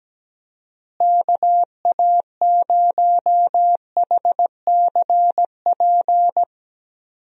Morse code, heard as a keyed tone, about 17 wpm, 700 Hz. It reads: KA0HCP
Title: USA Call Signs - 6 Characters - Repeated 3x - 17wpm